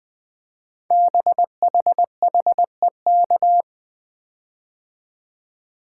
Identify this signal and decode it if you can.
Morse code, heard as a keyed tone, about 20 wpm, 700 Hz. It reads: BHHEK
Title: Code Groups - 5 Letters - 20wpm